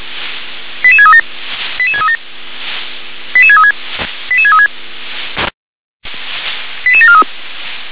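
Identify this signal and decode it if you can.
Five-tone selective call (ZVEI), heard as quick runs of five tones, tones of about 70 ms. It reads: 88637 88637 88636 88636 88743
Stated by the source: ZVEI five-tone selective call received on SDR.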